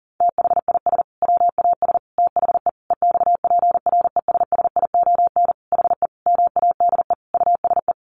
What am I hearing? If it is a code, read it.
Morse code, heard as a keyed tone, about 40 wpm, 700 Hz. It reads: THIS WAS THE EXPRESSION HE MADE USE
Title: Sentences from Top 1200 Words 40wpm